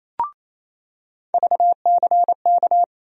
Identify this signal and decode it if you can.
Morse code, heard as a keyed tone, about 28 wpm, 700 Hz. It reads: VCK
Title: USA Call Sign Postfixes - 3 Letters 28wpm